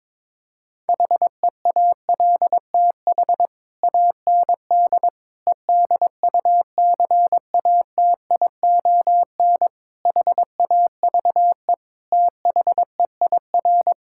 Morse code, heard as a keyed tone, about 22 wpm, 700 Hz. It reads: HEALTH AND EDUCATION HAVE THEIR
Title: Sentences from Top 1000 Words - Review Part 6 - 22wpm